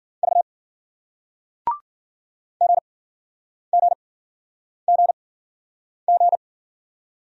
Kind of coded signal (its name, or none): Morse code